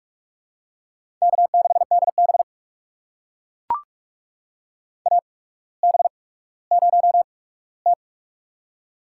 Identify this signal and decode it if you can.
Morse code, heard as a keyed tone, about 44 wpm, spaced as for 19 wpm, 700 Hz. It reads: K6DB AB0T
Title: USA Call Signs - 4 Characters - 8x Character Spacing - 45wpm